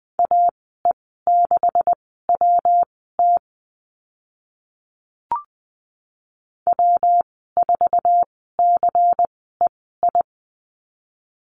Morse code, heard as a keyed tone, about 20 wpm, spaced as for 14 wpm, 700 Hz. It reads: AE6WT W4CEI
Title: 140 Lesson 23g - Letter f - Call Signs 20wpm